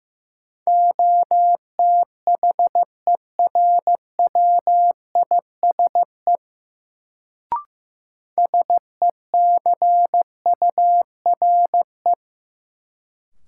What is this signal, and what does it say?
Morse code, heard as a keyed tone, about 15 wpm, 700 Hz. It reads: OTHERWISE SECURE
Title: Sentences from Top 1200 Words - New Word Drill Alt - 15wpm